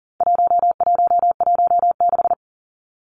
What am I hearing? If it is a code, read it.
Morse code, heard as a keyed tone, about 40 wpm, 700 Hz. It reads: 1116
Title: Numbers - 4-Digit 40wpm